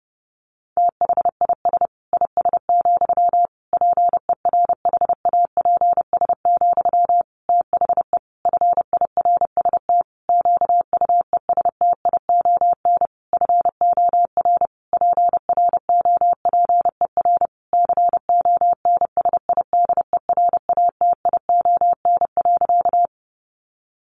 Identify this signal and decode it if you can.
Morse code, heard as a keyed tone, about 30 wpm, 700 Hz. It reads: THIS IS, PERHAPS, THE FIRST QUESTION FOR PROPER CONSIDERATION.
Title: Sentences from Top 1500 Words 30wpm